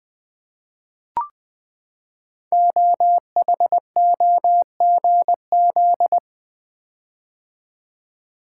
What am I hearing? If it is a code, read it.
Morse code, heard as a keyed tone, about 20 wpm, 700 Hz. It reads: OHOGZ
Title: Code Groups - 5 Letters - 20wpm